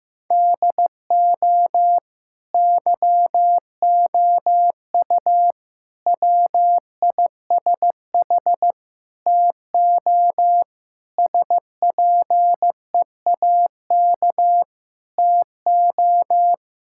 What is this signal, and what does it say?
Morse code, heard as a keyed tone, about 16 wpm, 700 Hz. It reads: DO YOU WISH TO SPEAK TO
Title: Sentences from Top 1600 Words 15wpm